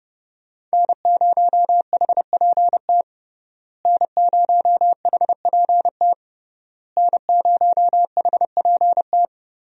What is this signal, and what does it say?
Morse code, heard as a keyed tone, about 30 wpm, 700 Hz. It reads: N0HPT N0HPT N0HPT
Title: USA Call Signs - 5 Characters - Repeated 3x - 30wpm